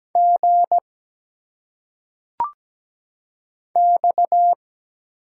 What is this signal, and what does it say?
Morse code, heard as a keyed tone, about 17 wpm, 700 Hz. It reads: G X
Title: Single Letters - Phonetic Pronunciation 17wpm